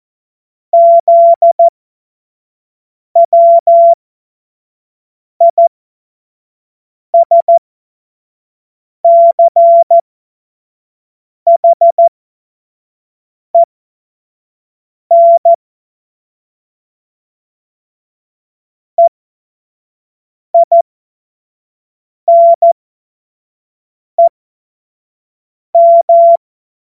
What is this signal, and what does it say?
Morse code, heard as a keyed tone, about 14 wpm, spaced as for 5 wpm, 680 Hz. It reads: ZWISCHEN EINEM
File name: Witz-14-Oktober-2025-25-F70